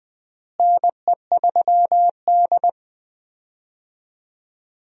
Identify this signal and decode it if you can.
Morse code, heard as a keyed tone, about 20 wpm, 700 Hz. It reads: NE3D
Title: USA Call Signs - 4 Characters - Repeated 2x - 20wpm